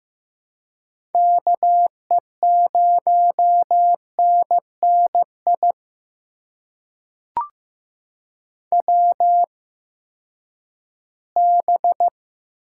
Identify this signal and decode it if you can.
Morse code, heard as a keyed tone, about 15 wpm, spaced as for 10 wpm, 700 Hz. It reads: KE0NNI WB
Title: USA Call Signs - 8x Character Spacing - 15wpm